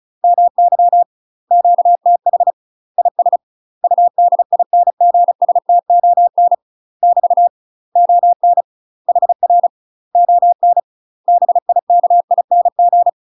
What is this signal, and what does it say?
Morse code, heard as a keyed tone, about 36 wpm, 700 Hz. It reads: MY QTH IS UDINGSTON <BT> ON HR ON BIKING
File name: UK MASTER QSOs TRAINING / 35WPM / UK MASTER QSOs 0011